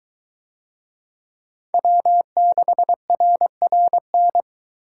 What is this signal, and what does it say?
Morse code, heard as a keyed tone, about 22 wpm, 700 Hz. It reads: W6RRN